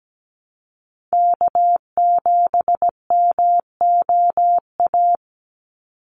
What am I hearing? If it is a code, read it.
Morse code, heard as a keyed tone, about 17 wpm, 700 Hz. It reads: K7MOA
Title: Call Signs - 17wpm